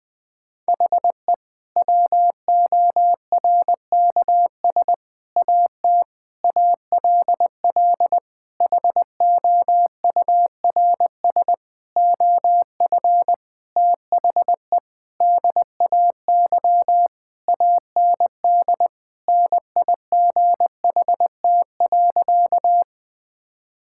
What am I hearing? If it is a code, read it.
Morse code, heard as a keyed tone, about 21 wpm, 700 Hz. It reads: HE WORKS AT ALL HOURS OF THE DAY AND NIGHT.
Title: Sentences with 8-Word Phrases - 20wpm